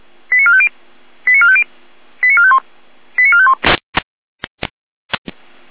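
Five-tone selective call (ZVEI), heard as quick runs of five tones, tones of about 70 ms. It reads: 89450 89450 89451 89451
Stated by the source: ZVEI five-tone selective call 89452 received with an SDR.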